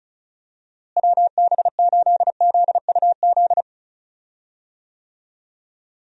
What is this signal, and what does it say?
Morse code, heard as a keyed tone, about 36 wpm, 700 Hz. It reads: WB8ZUZ
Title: USA Call Signs - 6 Characters - Repeated 2x - 35wpm